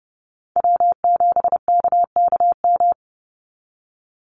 Morse code, heard as a keyed tone, about 30 wpm, 700 Hz. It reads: W7KKM
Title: NA SOTA Chaser Callsigns 30wpm